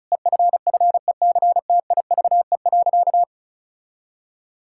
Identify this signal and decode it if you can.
Morse code, heard as a keyed tone, about 36 wpm, 700 Hz. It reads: EFFECTIVE.
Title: Sentences from Top 1000 Words - Review Part 6 - 35wpm